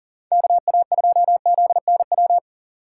Morse code, heard as a keyed tone, about 40 wpm, 700 Hz. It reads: KA2ZNW
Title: USA Call Signs - 6 Characters - 6x Character Spacing - 40wpm